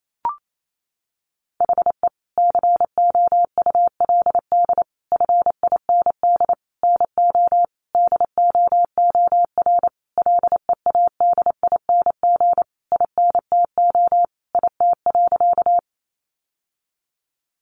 Morse code, heard as a keyed tone, about 28 wpm, 700 Hz. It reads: HE COULD FIND NO DOOR LEADING INTO IT.
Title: Sentences from Top 1100 Words 28wpm